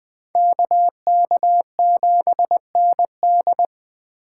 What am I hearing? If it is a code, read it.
Morse code, heard as a keyed tone, about 20 wpm, 700 Hz. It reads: KK7ND
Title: Speed Racing - USA 2x2 Call Signs 17wpm